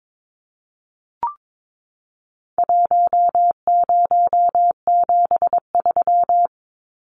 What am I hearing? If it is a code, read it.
Morse code, heard as a keyed tone, about 22 wpm, 700 Hz. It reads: 1073
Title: Numbers - 4-Digit 22wpm